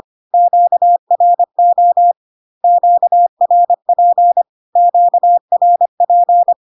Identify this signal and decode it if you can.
Morse code, heard as a keyed tone, about 25 wpm, 700 Hz. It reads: QRO QRP QRP
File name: Q-codes most common repeated 3 times 25 25wpm